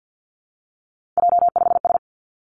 Morse code, heard as a keyed tone, about 50 wpm, 700 Hz. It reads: J5S
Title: Three Challenging Letter-Number Combinations 50wpm